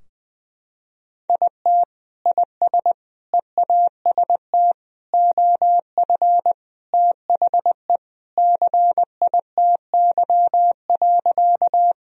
Morse code, heard as a keyed tone, about 20 wpm, 700 Hz. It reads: IT IS EAST OF THE CITY.
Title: Sentences from Top 1000 Words - Review Part 3 - 20wpm